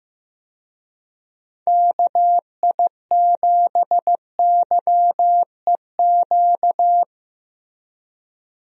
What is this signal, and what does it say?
Morse code, heard as a keyed tone, about 15 wpm, 700 Hz. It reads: KI7YEQ
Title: USA Call Signs - 6 Characters - Repeated 3x - 15wpm